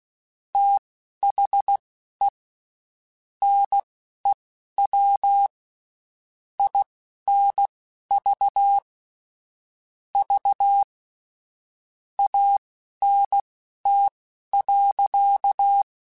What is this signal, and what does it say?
Morse code, heard as a keyed tone, about 16 wpm, spaced as for 11 wpm, 800 Hz. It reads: THE NEW INV V ANT.
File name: QSOs / QSOs-mp3s / WD0FBT 1@15